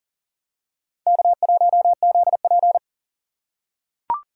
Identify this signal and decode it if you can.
Morse code, heard as a keyed tone, about 39 wpm, 700 Hz. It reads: K1ZP
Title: USA Call Signs - 4 Characters - 3x Character Spacing - 40wpm